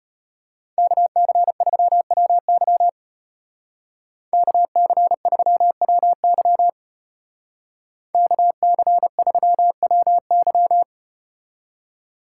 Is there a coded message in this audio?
Yes: Morse code from 0.8 to 10.8 s, heard as a keyed tone, about 32 wpm, 700 Hz. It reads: KC3WY KC3WY KC3WY